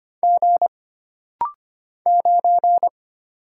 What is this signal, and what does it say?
Morse code, heard as a keyed tone, about 25 wpm, 700 Hz. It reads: G 9
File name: Single Character (Letter:Number:Punctuation:Prosign) - Rapid-Fire - 25wpm